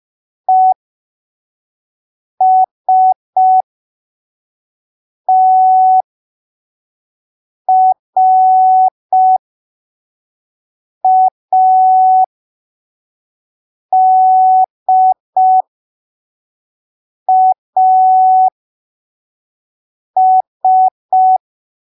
Morse code, heard as a keyed tone, about 5 wpm, spaced as for 3 wpm, 750 Hz. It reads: ESTRADAS